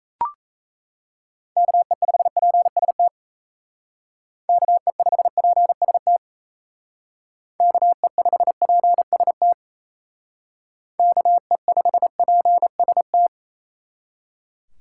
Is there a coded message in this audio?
Yes: Morse code from 1.6 to 13.3 s, heard as a keyed tone, about 36 wpm, 700 Hz. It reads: KE5PST KE5PST KE5PST KE5PST